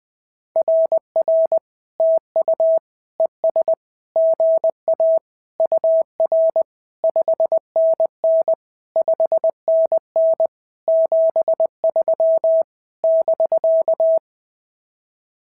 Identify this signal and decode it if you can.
Morse code, heard as a keyed tone, about 20 wpm, 650 Hz. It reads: RR TU ES GA UR 5NN 5NN 73 <BK>
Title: SOTA QSO Practice 20wpm